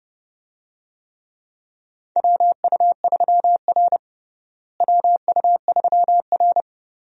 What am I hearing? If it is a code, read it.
Morse code, heard as a keyed tone, about 29 wpm, 700 Hz. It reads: WU3R WU3R
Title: USA Call Signs - 4 Characters - Repeated 2x - 30wpm